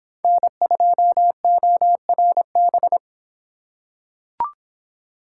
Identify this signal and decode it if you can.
Morse code, heard as a keyed tone, about 24 wpm, 700 Hz. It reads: N2ORB